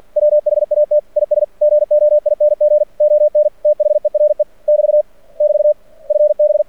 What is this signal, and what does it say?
Morse code, heard as a keyed tone, about 48 wpm, 600 Hz. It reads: WHAT IS GOING ON THERE <BT> <BT> FB
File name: archived / CRSnet